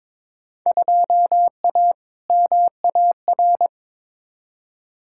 Morse code, heard as a keyed tone, about 22 wpm, 700 Hz. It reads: 2A MAR
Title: Field Day Exchanges - 22wpm